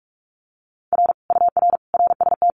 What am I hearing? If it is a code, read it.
Morse code, heard as a keyed tone, about 46 wpm, 700 Hz. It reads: R UR RST
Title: Common QSO Phrases Encore 45wpm